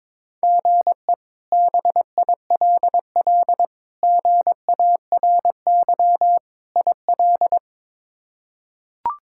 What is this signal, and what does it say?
Morse code, heard as a keyed tone, about 22 wpm, 700 Hz. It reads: GE BILL GARY IL
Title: K1USN SST QSO Practice - 22wpm